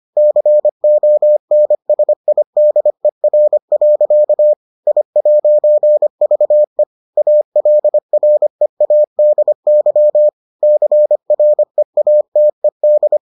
Morse code, heard as a keyed tone, about 25 wpm, 600 Hz. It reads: CONSIDER. I'VE ALREADY CREATED